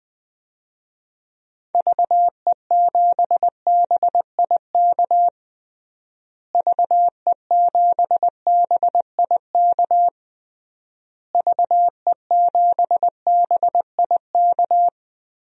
Morse code, heard as a keyed tone, about 19 wpm, 700 Hz. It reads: VE7BIK VE7BIK VE7BIK
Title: DXCC-100 Call Signs - 6 Characters - Repeated 3x - 20wpm